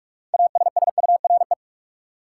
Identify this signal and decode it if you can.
Morse code, heard as a keyed tone, about 46 wpm, 700 Hz. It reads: ASSURE
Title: Six Letter Words - 45wpm